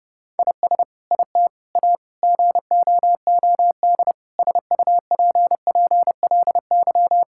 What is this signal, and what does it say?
Morse code, heard as a keyed tone, about 30 wpm, 700 Hz. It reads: IS IT A GOOD SUPPLY
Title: Sentences from Top 1000 Words - Review Part 5 - 30wpm